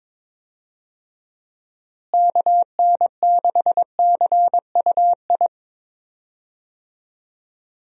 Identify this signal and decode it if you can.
Morse code, heard as a keyed tone, about 21 wpm, 700 Hz. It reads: KN6CUI
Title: USA Call Signs - 6 Characters - Repeated 3x - 22wpm